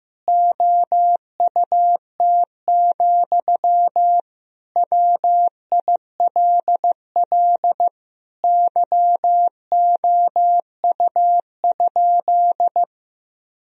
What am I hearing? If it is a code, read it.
Morse code, heard as a keyed tone, about 15 wpm, 700 Hz. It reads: OUT, WILL YOU?
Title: Sentences from Top 1000 Words - Review Part 1 - 15wpm